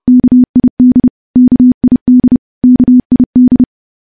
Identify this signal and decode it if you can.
Morse code, heard as a keyed tone, about 32 wpm, 250 Hz. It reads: KID KID KID
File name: morse / morse-training-30WPM